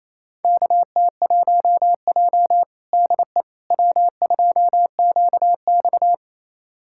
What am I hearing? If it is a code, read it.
Morse code, heard as a keyed tone, about 28 wpm, 700 Hz. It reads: KT1J DE W2QX
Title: USA Callsign DE Callsign - 4 Characters - 28wpm